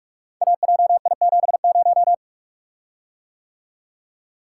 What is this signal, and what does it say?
Morse code, heard as a keyed tone, about 44 wpm, 700 Hz. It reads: AJI70
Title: Code Groups - 5 Letter:Numbers - 45wpm